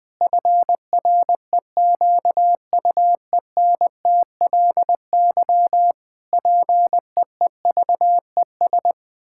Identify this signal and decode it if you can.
Morse code, heard as a keyed tone, about 20 wpm, 700 Hz. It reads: FREQUENTLY PEEVES
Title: Pangram Challenge v2 - 20wpm